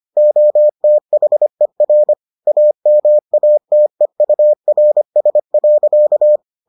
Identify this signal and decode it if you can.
Morse code, heard as a keyed tone, about 25 wpm, 600 Hz. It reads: OTHER AMATEURS.